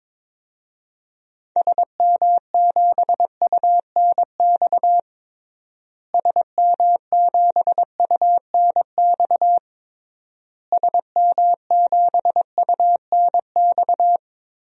Morse code, heard as a keyed tone, about 22 wpm, 700 Hz. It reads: SM7UNX SM7UNX SM7UNX
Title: DXCC-100 Call Signs - 6 Characters - Repeated 3x - 22wpm